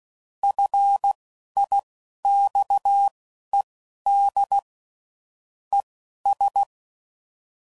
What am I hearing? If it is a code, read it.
Morse code, heard as a keyed tone, about 16 wpm, spaced as for 11 wpm, 800 Hz. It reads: FIXED ES
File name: CW Code Groups @10WPM / QSOs / QSO WA7QOX